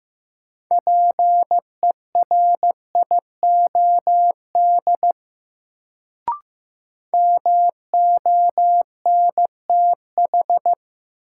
Morse code, heard as a keyed tone, about 15 wpm, 700 Hz. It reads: PERIOD MONTH
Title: Sentences from Top 800 Words - New Word Drill - 15wpm